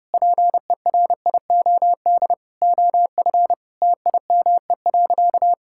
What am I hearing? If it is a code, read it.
Morse code, heard as a keyed tone, about 30 wpm, 700 Hz. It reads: PERIOD OF TIME.
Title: Sentences from Top 1000 Words - Review Part 5 - 30wpm